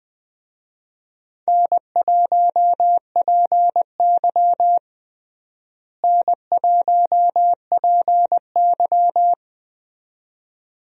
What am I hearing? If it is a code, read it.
Morse code, heard as a keyed tone, about 20 wpm, 700 Hz. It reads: N1PY N1PY
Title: USA Call Signs - 4 Characters - Repeated 2x - 20wpm